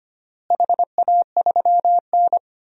Morse code, heard as a keyed tone, about 25 wpm, 700 Hz. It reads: HA3N
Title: DXCC-100 Call Signs - 4 Characters - Repeated 3x - 25wpm